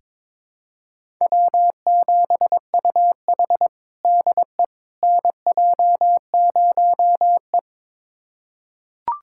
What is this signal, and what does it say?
Morse code, heard as a keyed tone, about 22 wpm, 700 Hz. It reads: W7UH DE NJ0E
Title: USA Callsign DE Callsign - 4 Characters - 22wpm